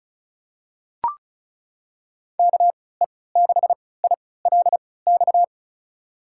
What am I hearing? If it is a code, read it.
Morse code, heard as a keyed tone, about 36 wpm, spaced as for 20 wpm, 700 Hz. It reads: KE6ILX
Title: USA Call Signs - 3x Character Spacing - 35wpm